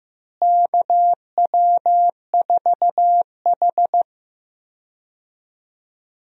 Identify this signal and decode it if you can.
Morse code, heard as a keyed tone, about 15 wpm, 700 Hz. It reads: KW4H
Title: USA Call Signs - 4 Characters - Repeated 3x - 15wpm